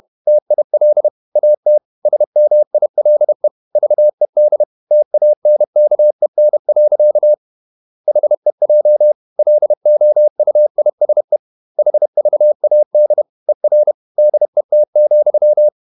Morse code, heard as a keyed tone, about 31 wpm, 600 Hz. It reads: TIL AT SMILE VED TANKEN. HEJ LOUISE HVAD ER DET,
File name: Ferie QRQ 31wpm